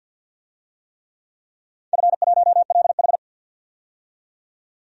Morse code, heard as a keyed tone, about 49 wpm, 700 Hz. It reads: F1LH
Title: DXCC-100 Call Signs - 4 Characters - 50wpm